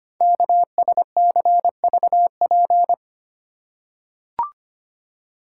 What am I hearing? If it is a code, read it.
Morse code, heard as a keyed tone, about 25 wpm, 700 Hz. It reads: KSCVP
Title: Code Groups - 5 Letters - 25wpm